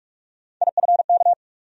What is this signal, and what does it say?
Morse code, heard as a keyed tone, about 46 wpm, 700 Hz. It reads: IFK
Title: USA Call Sign Postfixes - 3 Letters 45wpm